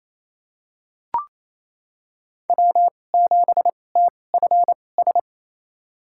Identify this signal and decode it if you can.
Morse code, heard as a keyed tone, about 28 wpm, spaced as for 20 wpm, 700 Hz. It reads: W7TFS
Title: USA Call Signs - 5 Characters - 2x Character Spacing - 28wpm